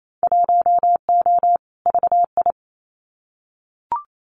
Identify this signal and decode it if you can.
Morse code, heard as a keyed tone, about 28 wpm, 700 Hz. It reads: 1O VI
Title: Winter Field Day Exchanges Encore 28wpm